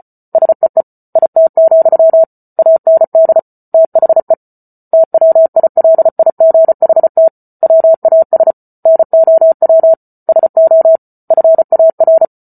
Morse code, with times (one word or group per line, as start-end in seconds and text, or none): 0.3-0.8 SEE
1.2-2.2 IT,
2.6-3.4 AND
3.7-4.3 THE
4.9-7.3 TWILIGHT
7.6-8.5 WAS
8.9-9.9 NOW
10.3-11.0 SO
11.3-12.2 FAR